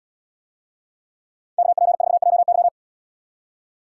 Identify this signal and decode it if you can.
Morse code, heard as a keyed tone, about 73 wpm, 700 Hz. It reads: KP4JY